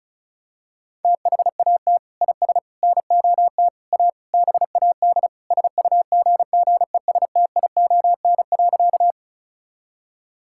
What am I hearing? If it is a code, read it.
Morse code, heard as a keyed tone, about 34 wpm, 700 Hz. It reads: THAT IS NOT A BAD SUGGESTION.
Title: Sentences from Top 1600 Words 35wpm